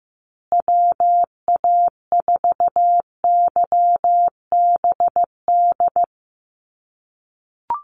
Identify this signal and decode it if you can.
Morse code, heard as a keyed tone, about 15 wpm, 700 Hz. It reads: WA4YBD
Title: USA Call Signs Repeated 3x 15wpm